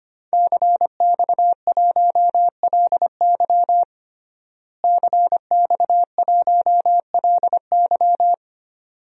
Morse code, heard as a keyed tone, about 25 wpm, 700 Hz. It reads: CX1LY CX1LY
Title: DXCC-100 Call Signs - 5 Characters - Repeated 2x - 25wpm